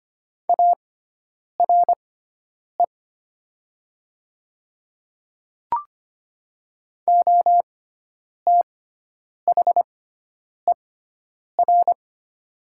Morse code, heard as a keyed tone, about 25 wpm, spaced as for 9 wpm, 700 Hz. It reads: ARE OTHER
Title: 6x Character Spacing - Top 100 Words - 25wpm